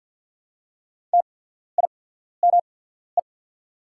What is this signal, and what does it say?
Morse code, heard as a keyed tone, about 52 wpm, spaced as for 14 wpm, 700 Hz. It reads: TIME
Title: 8x Character Spacing - Top 100 Words - 50wpm